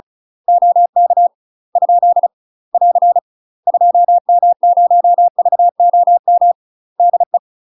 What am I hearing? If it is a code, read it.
Morse code, heard as a keyed tone, about 36 wpm, 700 Hz. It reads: OK ? <AR> 2M0VOM DE
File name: UK MASTER QSOs TRAINING / 35WPM / UK MASTER QSOs 0031